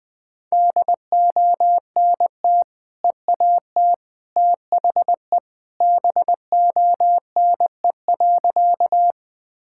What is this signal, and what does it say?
Morse code, heard as a keyed tone, about 20 wpm, 700 Hz. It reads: DONT EAT THE BONE.
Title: Sentences from Top 1600 Words 20wpm